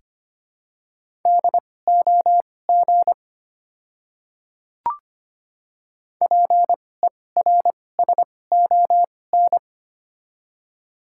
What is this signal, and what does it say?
Morse code, heard as a keyed tone, about 25 wpm, spaced as for 18 wpm, 700 Hz. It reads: DOG PERSON
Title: 2x Character Spacing - Top 500 Words - 25wpm